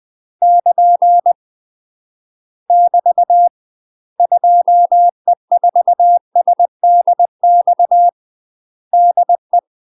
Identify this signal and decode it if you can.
Morse code, heard as a keyed tone, about 20 wpm, 700 Hz. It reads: <KN> <BT> 2E4SDX DE
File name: UK MASTER QSOs TRAINING / 20WPM / UK MASTER QSOs 0017